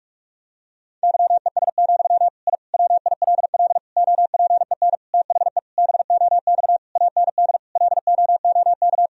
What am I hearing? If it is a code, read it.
Morse code, heard as a keyed tone, about 46 wpm, 700 Hz. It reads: YES, I WILL OPEN THE BOX AND LOOK
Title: Sentences from Top 1000 Words - Review Part 2 - 45wpm